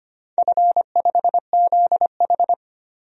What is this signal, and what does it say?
Morse code, heard as a keyed tone, about 25 wpm, 700 Hz. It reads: F5ZH
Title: DXCC-100 Call Signs - 4 Characters - 25wpm